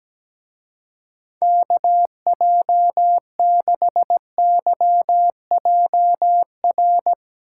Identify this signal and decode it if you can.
Morse code, heard as a keyed tone, about 18 wpm, 700 Hz. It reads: KJ6YJR
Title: Speed Racing - USA 2x3 Call Signs 17wpm